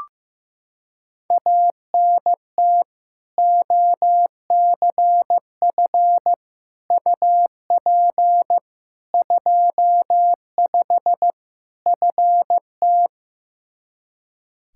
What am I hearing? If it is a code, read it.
Morse code, heard as a keyed tone, about 15 wpm, 700 Hz. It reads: ANT OCF UP 25 FT
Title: Practice QSO - Exchange 2 - Info - 15wpm